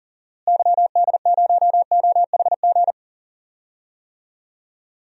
Morse code, heard as a keyed tone, about 40 wpm, 700 Hz. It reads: YD0OHG
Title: DXCC-100 Call Signs - 6 Characters - Repeated 2x - 40wpm